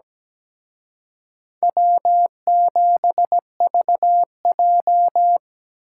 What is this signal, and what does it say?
Morse code, heard as a keyed tone, about 20 wpm, 700 Hz. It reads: W7VJ
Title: Speed Racing - USA Call Signs 17wpm